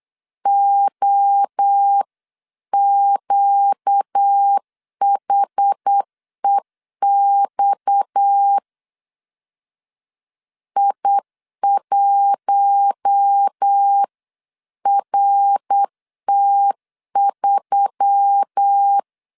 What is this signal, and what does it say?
Morse code, heard as a keyed tone, about 8 wpm, 800 Hz. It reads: OQHEX I1RT3
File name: 8wpm